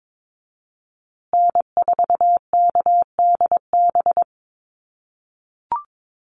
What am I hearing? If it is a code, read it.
Morse code, heard as a keyed tone, about 22 wpm, 700 Hz. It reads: N4KDB
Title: USA Call Signs Repeated 2x 22wpm